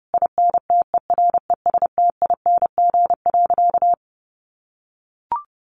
Morse code, heard as a keyed tone, about 28 wpm, 700 Hz. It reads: INTERESTING.
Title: Sentences from Top 1200 Words 30wpm